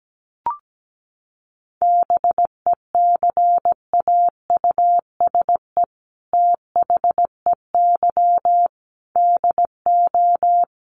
Morse code, heard as a keyed tone, about 17 wpm, 700 Hz. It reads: BECAUSE THEY DO
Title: Sets of 5 Words Encore2 17wpm